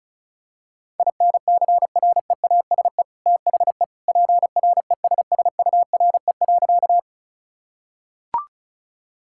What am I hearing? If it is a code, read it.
Morse code, heard as a keyed tone, about 34 wpm, 700 Hz. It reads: INCREASE THE PRESSURE.
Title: Sentences from Top 1000 Words - Review Part 6 - 35wpm